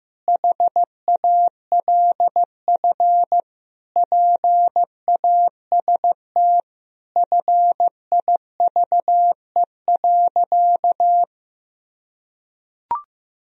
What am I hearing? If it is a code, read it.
Morse code, heard as a keyed tone, about 15 wpm, 700 Hz. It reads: HALF PAST FIVE.
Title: Sentences from Top 1000 Words - Review Part 3 - 15wpm